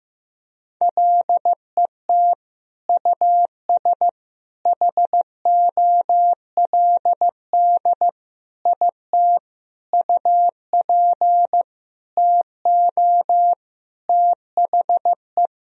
Morse code, heard as a keyed tone, about 15 wpm, 700 Hz. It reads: LET US HOLD IT UP TO THE
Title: Sentences from Top 1000 Words - Review Part 2 - 15wpm